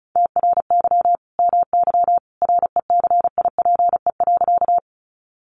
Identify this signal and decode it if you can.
Morse code, heard as a keyed tone, about 34 wpm, 700 Hz. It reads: TRY MY RECIPE.
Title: Sentences from Top 1200 Words 35wpm